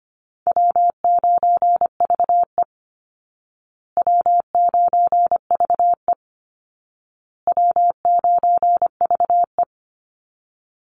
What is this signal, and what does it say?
Morse code, heard as a keyed tone, about 25 wpm, 700 Hz. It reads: W9VE W9VE W9VE
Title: Easy Call Signs Repeated 3x - 25wpm